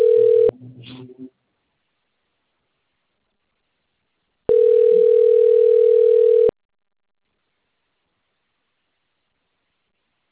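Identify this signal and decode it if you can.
Ringback tone (North America), 440 + 480 Hz.
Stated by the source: Outbound phone call recorded on the Limeira system.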